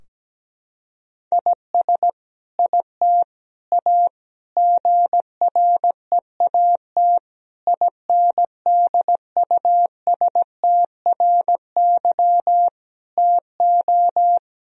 Morse code, heard as a keyed tone, about 17 wpm, 700 Hz. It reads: IS IT A GREAT INDUSTRY TO
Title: Sentences from Top 1000 Words - Review Part 4 - 17wpm